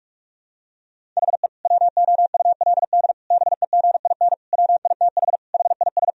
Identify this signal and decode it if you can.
Morse code, heard as a keyed tone, about 46 wpm, 700 Hz. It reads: HE WOULD BEGIN WITH HIS
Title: Sentences from Top 1000 Words - Review Part 1 - 45wpm